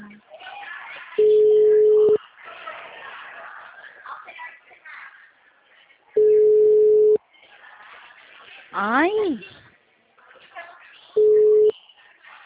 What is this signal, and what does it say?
Ringback tone (Europe, 425 Hz), with speech.
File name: out- -2017- - - .15539